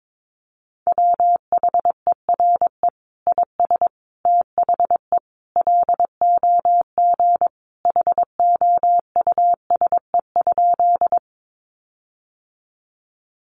Morse code, heard as a keyed tone, about 22 wpm, 700 Hz. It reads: WHERE IS THE LOG HOUSE?
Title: Sentences from Top 1400 Words 22wpm